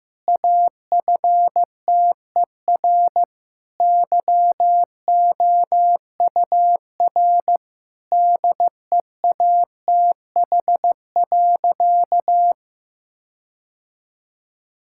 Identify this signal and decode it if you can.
Morse code, heard as a keyed tone, about 15 wpm, 700 Hz. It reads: AFTER YOUR DEATH.
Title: Sentences from Top 1600 Words 15wpm